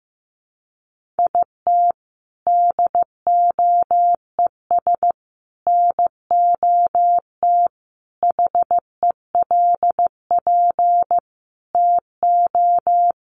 Morse code, heard as a keyed tone, about 15 wpm, 700 Hz. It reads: IT DOES NOT HELP TO
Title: Sentences from Top 1500 Words 15wpm